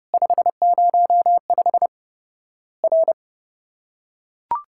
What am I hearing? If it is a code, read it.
Morse code, heard as a keyed tone, about 30 wpm, 700 Hz. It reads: 505 R
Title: Worked All Europe DX Contest - QTC Exchange - 30wpm